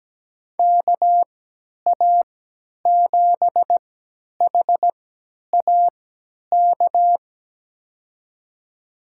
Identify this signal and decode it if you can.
Morse code, heard as a keyed tone, about 17 wpm, spaced as for 10 wpm, 700 Hz. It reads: KA7HAK
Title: USA Call Signs - 3x Character Spacing - 17wpm